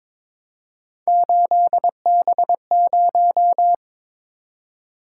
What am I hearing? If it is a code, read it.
Morse code, heard as a keyed tone, about 27 wpm, 700 Hz. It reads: 8B0
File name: Speed-Racing - Three Letter-Number Combinations - 22wpm